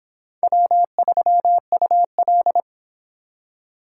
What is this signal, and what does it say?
Morse code, heard as a keyed tone, about 26 wpm, 700 Hz. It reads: W3UL